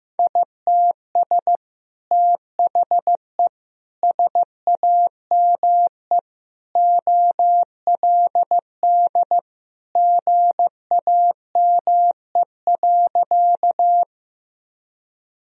Morse code, heard as a keyed tone, about 15 wpm, 700 Hz. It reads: ITS THE SAME OLD GAME.
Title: Sentences from Top 1000 Words - Review Part 3 - 15wpm